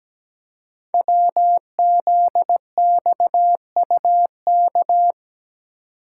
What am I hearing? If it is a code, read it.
Morse code, heard as a keyed tone, about 17 wpm, 700 Hz. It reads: WZXUK
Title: Code Groups - 5 Letter:Numbers - 17wpm